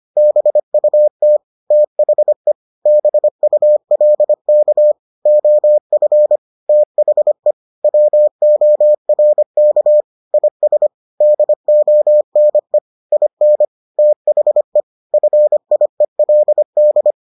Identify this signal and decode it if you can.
Morse code, heard as a keyed tone, about 25 wpm, 600 Hz. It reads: BUT THE BULK OF THE WORK IS DONE IN THE FIELD